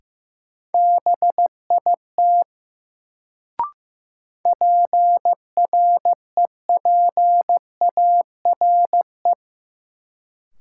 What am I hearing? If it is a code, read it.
Morse code, heard as a keyed tone, about 15 wpm, 700 Hz. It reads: BIT PREPARE
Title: Sentences from Top 900 Words - New Word Drill - 15wpm